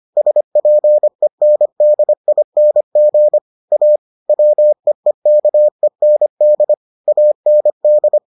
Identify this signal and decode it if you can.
Morse code, heard as a keyed tone, about 25 wpm, 600 Hz. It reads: SPENDING A WEEKEND AND